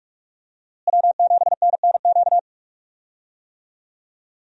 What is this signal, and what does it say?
Morse code, heard as a keyed tone, about 46 wpm, 700 Hz. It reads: W7NNQ
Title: USA Call Signs - 5 Characters - Repeated 2x - 45wpm